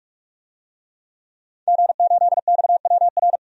Morse code, heard as a keyed tone, about 42 wpm, 700 Hz. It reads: G8XWR
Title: DXCC-100 Call Signs - 5 Characters - 45wpm